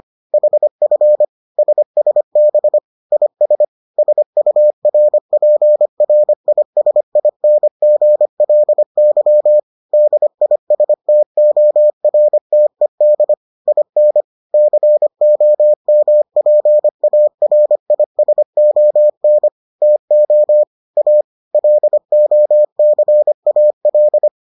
Morse code, heard as a keyed tone, about 25 wpm, 600 Hz. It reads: HF SSB IS SURPRISINGLY DISTORTED IN COMPARISON TO A LOCAL